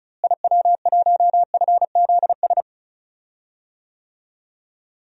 Morse code, heard as a keyed tone, about 34 wpm, 700 Hz. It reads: IW1FZS
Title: DXCC-100 Call Signs - 6 Characters - Repeated 2x - 35wpm